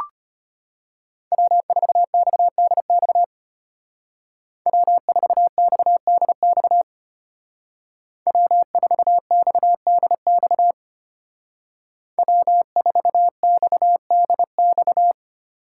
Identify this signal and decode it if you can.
Morse code, heard as a keyed tone, about 30 wpm, 700 Hz. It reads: W4XDX W4XDX W4XDX W4XDX
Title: Speed Racing - USA 1x3 Call Signs 25wpm